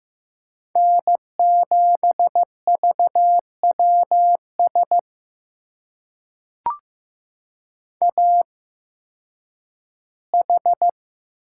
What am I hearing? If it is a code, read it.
Morse code, heard as a keyed tone, about 15 wpm, spaced as for 10 wpm, 700 Hz. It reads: N7VWS AH
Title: USA Call Signs - 5 Characters - 8x Character Spacing - 15wpm